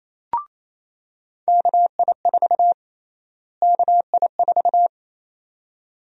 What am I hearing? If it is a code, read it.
Morse code, heard as a keyed tone, about 28 wpm, 700 Hz. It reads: KI4 KI4
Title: USA Call Sign Prefixes - Repeated 2x - 28wpm